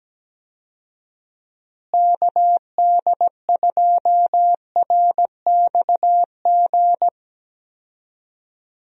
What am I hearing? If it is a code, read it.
Morse code, heard as a keyed tone, about 17 wpm, 700 Hz. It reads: KD2RXG
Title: USA Call Signs - 6 Characters - Repeated 2x - 17wpm